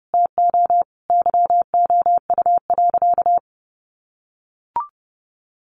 Morse code, heard as a keyed tone, about 30 wpm, 700 Hz. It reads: TO YOU.
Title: Sentences from Top 1000 Words 30wpm